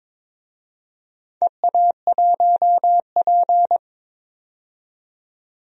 Morse code, heard as a keyed tone, about 22 wpm, 700 Hz. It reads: EA1P
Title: DXCC-100 Call Signs - 4 Characters - 22wpm